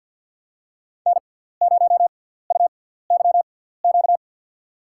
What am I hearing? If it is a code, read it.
Morse code, heard as a keyed tone, about 47 wpm, spaced as for 17 wpm, 700 Hz. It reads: N0UYQ
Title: USA Call Signs - 5 Characters - 6x Character Spacing - 50wpm